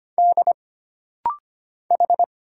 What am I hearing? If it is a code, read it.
Morse code, heard as a keyed tone, about 25 wpm, 700 Hz. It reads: D H
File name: Single Character (Letter:Number:Punctuation:Prosign) - Rapid-Fire - 25wpm